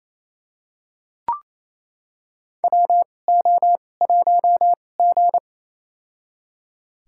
Morse code, heard as a keyed tone, about 28 wpm, spaced as for 20 wpm, 700 Hz. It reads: WO1G
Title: USA Call Signs - 4 Characters - 2x Character Spacing - 28wpm